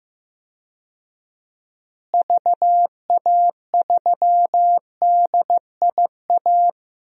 Morse code, heard as a keyed tone, about 15 wpm, 700 Hz. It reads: VA3DIA
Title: DXCC-100 Call Signs - 6 Characters - Repeated 2x - 15wpm